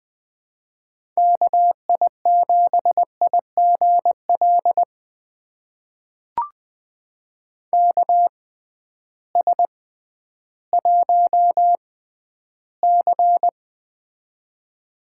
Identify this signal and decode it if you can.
Morse code, heard as a keyed tone, about 20 wpm, spaced as for 12 wpm, 700 Hz. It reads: KI7IGL KS1C
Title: USA Call Signs - 6x Character Spacing - 20wpm